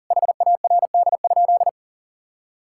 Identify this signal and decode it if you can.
Morse code, heard as a keyed tone, about 40 wpm, 700 Hz. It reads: HARD?
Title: Sentences from Top 1000 Words - Review Part 5 - 40wpm